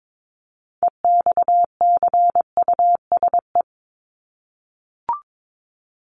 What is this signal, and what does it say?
Morse code, heard as a keyed tone, about 22 wpm, 700 Hz. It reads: EXCUSE
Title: Top 501-750 Words 22wpm